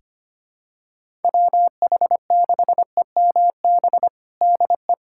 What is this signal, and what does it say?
Morse code, heard as a keyed tone, about 25 wpm, 700 Hz. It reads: WH6EMB DE
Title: USA Callsign DE Callsign 25wpm